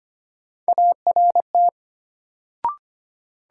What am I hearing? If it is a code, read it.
Morse code, heard as a keyed tone, about 25 wpm, 700 Hz. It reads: ART
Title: Sentences from Top 600 Words - New Word Drill - 25wpm